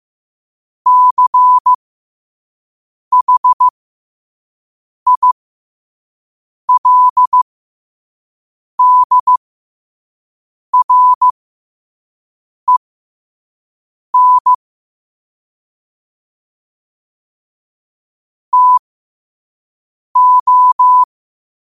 Morse code, heard as a keyed tone, about 15 wpm, spaced as for 5 wpm, 1000 Hz. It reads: CHILDREN TO
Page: New quotes every day in morse code at 5 Words per minute.